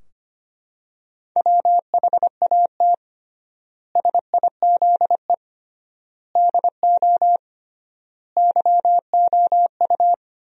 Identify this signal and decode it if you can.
Morse code, heard as a keyed tone, about 25 wpm, 700 Hz. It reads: WHAT SIZE DO YOU
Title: 3x Word-Spacing - Sentences from Top 400 Words - 25wpm